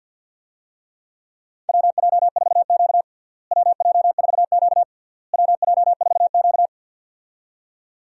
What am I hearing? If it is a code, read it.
Morse code, heard as a keyed tone, about 50 wpm, 700 Hz. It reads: WJ4Q WJ4Q WJ4Q
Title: USA Call Signs - 4 Characters - Repeated 3x - 50wpm